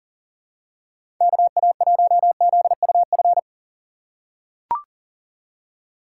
Morse code, heard as a keyed tone, about 40 wpm, 700 Hz. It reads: KA1ZUF
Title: USA Call Signs - 6 Characters - 4x Character Spacing - 40wpm